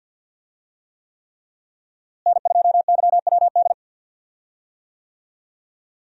Morse code, heard as a keyed tone, about 53 wpm, 700 Hz. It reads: N2YWD
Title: USA Call Signs - 5 Characters - Repeated 3x - 50wpm